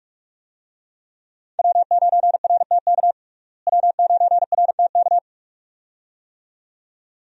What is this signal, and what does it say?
Morse code, heard as a keyed tone, about 41 wpm, 700 Hz. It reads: W9RTK W9RTK
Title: USA Call Signs - 5 Characters - Repeated 2x - 45wpm